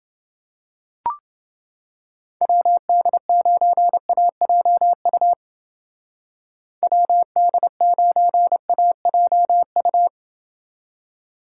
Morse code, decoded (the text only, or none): WD9AJU WD9AJU